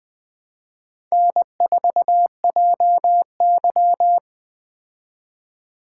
Morse code, heard as a keyed tone, about 20 wpm, 700 Hz. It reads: N4JY
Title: Speed Racing - USA Call Signs 17wpm